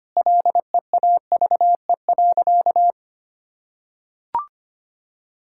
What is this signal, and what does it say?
Morse code, heard as a keyed tone, about 25 wpm, 700 Hz. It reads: LEAVE.
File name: practice / 25wpm / Sentences with 4-Word Phrases - 25wpm